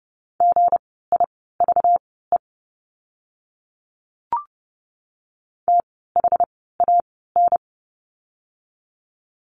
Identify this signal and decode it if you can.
Morse code, heard as a keyed tone, about 30 wpm, spaced as for 17 wpm, 700 Hz. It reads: GIVE THAN
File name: course / 30:10wpm / 237 Lesson 38f - Letter q - Words 30wpm